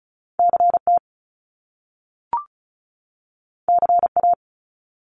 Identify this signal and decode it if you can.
Morse code, heard as a keyed tone, about 36 wpm, 700 Hz. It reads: CT CA
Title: US States 35wpm